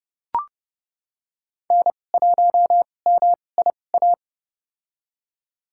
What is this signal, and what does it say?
Morse code, heard as a keyed tone, about 30 wpm, spaced as for 22 wpm, 700 Hz. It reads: N1MIA
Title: USA Call Signs - 5 Characters - 2x Character Spacing - 30wpm